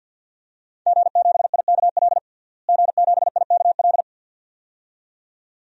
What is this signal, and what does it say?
Morse code, heard as a keyed tone, about 52 wpm, 700 Hz. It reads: G7IKL G7IKL
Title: DXCC-100 Call Signs - 5 Characters - Repeated 2x - 50wpm